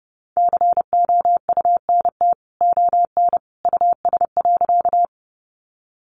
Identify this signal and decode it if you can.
Morse code, heard as a keyed tone, about 30 wpm, 700 Hz. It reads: COUNT ON US.
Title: Sentences from Top 1500 Words 30wpm